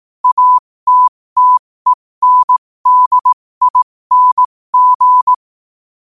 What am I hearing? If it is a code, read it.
Morse code, heard as a keyed tone, about 18 wpm, spaced as for 15 wpm, 1000 Hz. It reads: ATTENDING
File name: qotdfiles / 15wpm